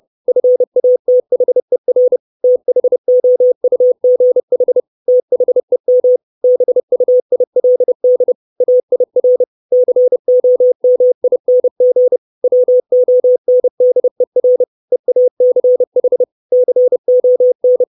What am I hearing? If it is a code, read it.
Morse code, heard as a keyed tone, about 30 wpm, 500 Hz. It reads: FATHER THOUGH THEM BUILD AIR COMING WONDER EACH CON
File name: 632 Words 30WPM